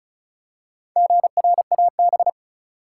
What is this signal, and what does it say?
Morse code, heard as a keyed tone, about 36 wpm, 700 Hz. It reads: GRAB
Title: Four Letter Words - 35wpm